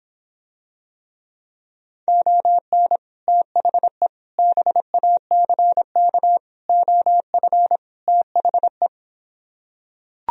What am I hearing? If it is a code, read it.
Morse code, heard as a keyed tone, about 26 wpm, 700 Hz. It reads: ON THE BACK OF THE